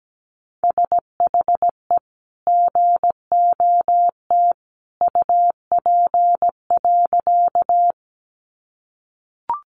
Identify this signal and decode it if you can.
Morse code, heard as a keyed tone, about 17 wpm, 700 Hz. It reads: SHE GOT UP.
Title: Sentences from Top 300 Words 17wpm